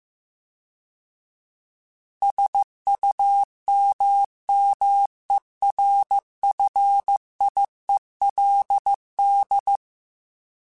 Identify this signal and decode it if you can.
Morse code, heard as a keyed tone, about 15 wpm, 800 Hz. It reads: SUMMERFIELD